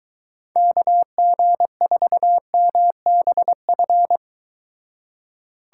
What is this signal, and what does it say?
Morse code, heard as a keyed tone, about 23 wpm, 700 Hz. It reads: KG4MBF